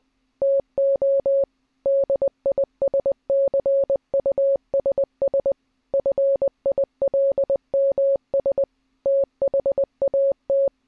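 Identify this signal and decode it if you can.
Morse code, heard as a keyed tone, about 20 wpm, 550 Hz. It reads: TO DISCUSS FILMS THAT
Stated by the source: Morse clip taken from the film Catch Me If You Can.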